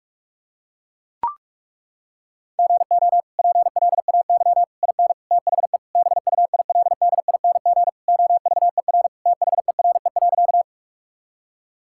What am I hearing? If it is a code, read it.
Morse code, heard as a keyed tone, about 42 wpm, 700 Hz. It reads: GO PLAY IN THE BUILDING OVER THERE.
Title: Sentences from Top 1000 Words - Review Part 5 - 45wpm